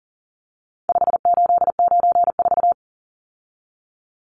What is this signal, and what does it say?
Morse code, heard as a keyed tone, about 40 wpm, 700 Hz. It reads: 5894
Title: Numbers - 4-Digit 40wpm